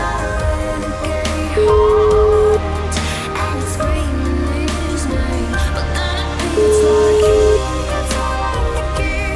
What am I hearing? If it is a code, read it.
Ringback tone (Europe, 425 Hz), with noise.